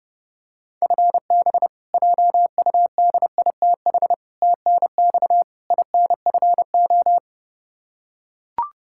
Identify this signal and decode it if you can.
Morse code, heard as a keyed tone, about 32 wpm, 700 Hz. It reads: FB JUDITH TNX INFO
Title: K1USN SST QSO Practice 30wpm